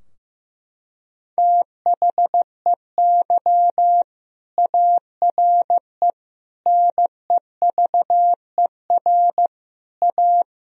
Morse code, heard as a keyed tone, about 16 wpm, 700 Hz. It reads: THEY ARE NEVER A
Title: Sentences from Top 1000 Words - Review Part 4 - 15wpm